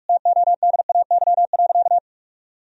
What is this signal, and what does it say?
Morse code, heard as a keyed tone, about 46 wpm, 700 Hz. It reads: TODAY.
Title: Sentences from Top 1000 Words - Review Part 4 - 45wpm